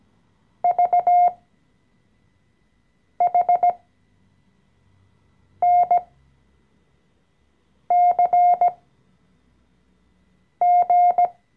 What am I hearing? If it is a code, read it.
Morse code, heard as a keyed tone, about 17 wpm, spaced as for 4 wpm, 710 Hz. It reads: VHNCG
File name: CW-Online-Kurs / Anfänger / Lektion 014